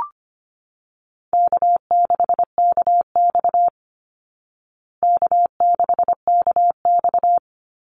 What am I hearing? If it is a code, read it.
Morse code, heard as a keyed tone, about 25 wpm, 700 Hz. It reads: K6KX K6KX
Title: Easy Call Signs Repeated 2x - 25wpm